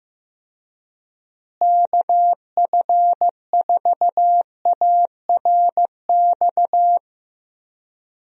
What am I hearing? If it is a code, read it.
Morse code, heard as a keyed tone, about 15 wpm, 700 Hz. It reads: KF4ARX
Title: USA Call Signs - 6 Characters - Repeated 2x - 15wpm